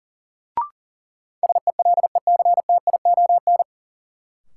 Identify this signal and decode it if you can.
Morse code, heard as a keyed tone, about 37 wpm, 700 Hz. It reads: SELECTION
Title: Sentences from Top 1400 Words - New Word Drill - 40wpm